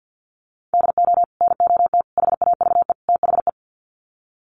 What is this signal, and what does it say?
Morse code, heard as a keyed tone, about 48 wpm, 700 Hz. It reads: DO NOT HAVE THE
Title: Sets of 4 Words 50wpm